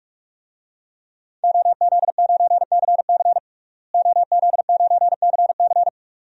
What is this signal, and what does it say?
Morse code, heard as a keyed tone, about 42 wpm, 700 Hz. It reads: OZ9CC OZ9CC
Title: DXCC-100 Call Signs - 5 Characters - Repeated 2x - 45wpm